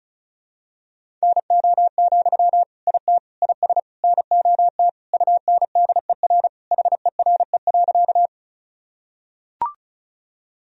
Morse code, heard as a keyed tone, about 36 wpm, 700 Hz. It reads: NO, IT IS NOT UNDER HERE.
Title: Sentences from Top 1000 Words - Review Part 1 - 35wpm